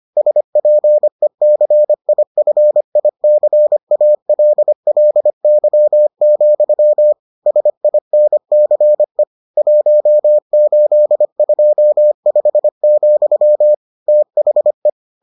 Morse code, heard as a keyed tone, about 25 wpm, 600 Hz. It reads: SPECIFICALLY, SINCE 1825, THE